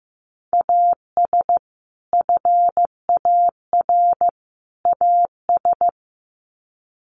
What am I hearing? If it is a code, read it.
Morse code, heard as a keyed tone, about 15 wpm, 700 Hz. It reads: AS FAR AS
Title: Sets of 3 Words 15wpm